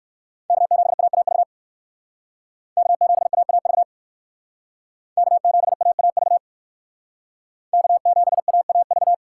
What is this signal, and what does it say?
Morse code, heard as a keyed tone, about 57 wpm, 700 Hz. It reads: K7AAV K7AAV K7AAV K7AAV